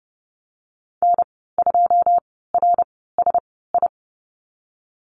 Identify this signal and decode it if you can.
Morse code, heard as a keyed tone, about 30 wpm, spaced as for 17 wpm, 700 Hz. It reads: N2RSI
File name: Lesson 14g - Number 2 - Call Signs 30wpm